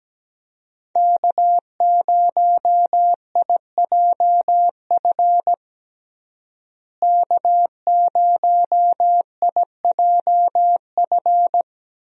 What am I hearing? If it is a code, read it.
Morse code, heard as a keyed tone, about 17 wpm, 700 Hz. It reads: K0IJF K0IJF
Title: USA Call Signs - 5 Characters - Repeated 3x - 17wpm